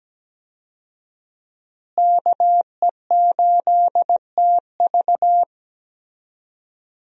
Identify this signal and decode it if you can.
Morse code, heard as a keyed tone, about 17 wpm, 700 Hz. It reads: KE8TV
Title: USA Call Signs - 5 Characters - Repeated 2x - 17wpm